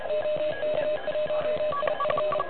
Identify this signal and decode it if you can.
Morse code, heard as a keyed tone, about 32 wpm, 600 Hz, with noise. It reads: OK1IF
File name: truSDX / NZ contest